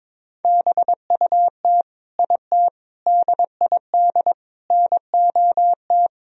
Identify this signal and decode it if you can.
Morse code, heard as a keyed tone, about 22 wpm, 700 Hz. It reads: BUT IT DID NOT
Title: Sentences from Top 1000 Words - Review Part 2 - 22wpm